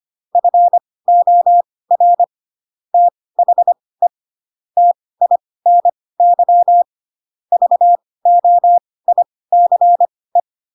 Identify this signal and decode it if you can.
Morse code, heard as a keyed tone, about 25 wpm, spaced as for 18 wpm, 700 Hz. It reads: FOR THE TINY VOICE
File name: Pinocchio / Pinocchio ch1 12 25 ik1hge